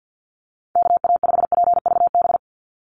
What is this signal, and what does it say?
Morse code, heard as a keyed tone, about 50 wpm, 700 Hz. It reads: KA5PVB
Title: NA SOTA Chaser Callsigns 50wpm